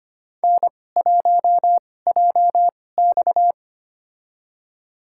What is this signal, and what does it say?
Morse code, heard as a keyed tone, about 25 wpm, spaced as for 18 wpm, 700 Hz. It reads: N1JX
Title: USA Call Signs - 4 Characters - 2x Character Spacing - 25wpm